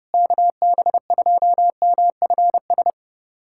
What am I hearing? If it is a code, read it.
Morse code, heard as a keyed tone, about 30 wpm, 700 Hz. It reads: KB2MFS
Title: USA Callsign DE Callsign - 6 Characters - 30wpm